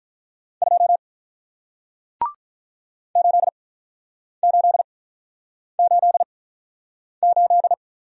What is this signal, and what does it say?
Morse code, heard as a keyed tone, about 48 wpm, spaced as for 9 wpm, 700 Hz. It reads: 2 8888